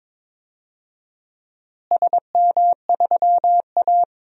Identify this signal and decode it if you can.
Morse code, heard as a keyed tone, about 21 wpm, 700 Hz. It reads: SM3A
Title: DXCC-100 Call Signs - 4 Characters - Repeated 3x - 22wpm